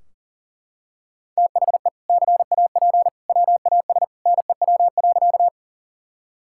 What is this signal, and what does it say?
Morse code, heard as a keyed tone, about 40 wpm, 700 Hz. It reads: THE CAP WAS NEW.
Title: Sentences from Top 1600 Words 40wpm